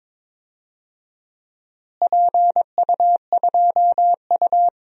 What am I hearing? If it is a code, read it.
Morse code, heard as a keyed tone, about 21 wpm, 700 Hz. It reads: PU2U
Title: DXCC-100 Call Signs - 4 Characters - Repeated 3x - 22wpm